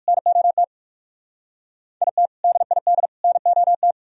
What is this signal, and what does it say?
Morse code, heard as a keyed tone, about 58 wpm, 700 Hz. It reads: NOT IT DID NOT